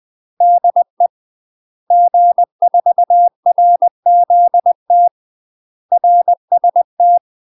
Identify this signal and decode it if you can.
Morse code, heard as a keyed tone, about 20 wpm, 700 Hz. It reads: DE G4RZT RST
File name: UK MASTER QSOs TRAINING / 20WPM / UK MASTER QSOs 0044